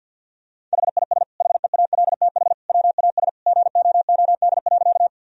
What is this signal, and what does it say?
Morse code, heard as a keyed tone, about 50 wpm, 700 Hz. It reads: HIS HEALTH WAS GOOD.
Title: Sentences from Top 1000 Words - Review Part 3 - 50wpm